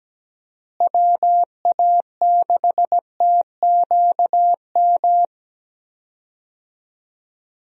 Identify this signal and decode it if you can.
Morse code, heard as a keyed tone, about 17 wpm, 700 Hz. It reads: WA6TQM
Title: USA Call Signs - 6 Characters - Repeated 2x - 17wpm